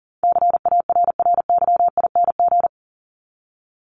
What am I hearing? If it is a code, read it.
Morse code, heard as a keyed tone, about 37 wpm, 700 Hz. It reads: CARRYING
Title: ING Suffix - Top 100 Words with Base Words 40wpm